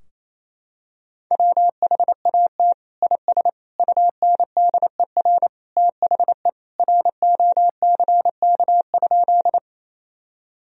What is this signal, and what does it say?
Morse code, heard as a keyed tone, about 28 wpm, 700 Hz. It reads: WHAT IS UNDER THE ROCK?
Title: Sentences from Top 1000 Words - Review Part 2 - 28wpm